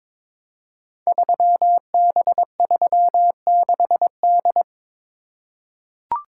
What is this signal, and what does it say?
Morse code, heard as a keyed tone, about 21 wpm, 700 Hz. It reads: 3B36D
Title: Code Groups - 5 Letter:Numbers - 22wpm